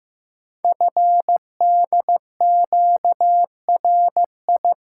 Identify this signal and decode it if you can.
Morse code, heard as a keyed tone, about 16 wpm, 700 Hz. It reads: FDQRI
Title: Code Groups - 5 Letters - 15wpm